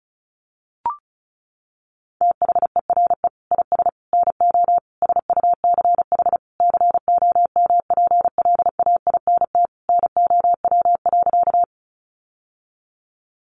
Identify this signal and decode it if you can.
Morse code, heard as a keyed tone, about 34 wpm, 700 Hz. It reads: THERE IS NO SUCH COMPLAINT NOW.
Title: Sentences from Top 1400 Words 35wpm